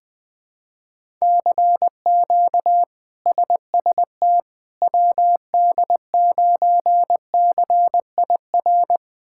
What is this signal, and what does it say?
Morse code, heard as a keyed tone, about 20 wpm, 700 Hz. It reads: CQ SST WD9CIR
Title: K1USN SST QSO Practice - 20wpm